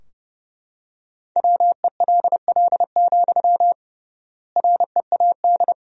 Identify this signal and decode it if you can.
Morse code, heard as a keyed tone, about 29 wpm, 700 Hz. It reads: WELL, READ
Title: 3x Word-Spacing - Sentences from Top 200 Words - 30wpm